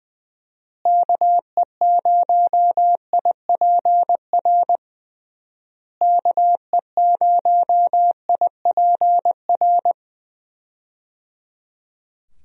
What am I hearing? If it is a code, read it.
Morse code, heard as a keyed tone, about 20 wpm, 700 Hz. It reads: KE0IPR KE0IPR
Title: USA Call Signs - 6 Characters - Repeated 3x - 20wpm